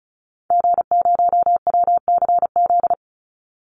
Morse code, heard as a keyed tone, about 34 wpm, 700 Hz. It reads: G0WCZ
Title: Call Signs Repeated 3x - 35wpm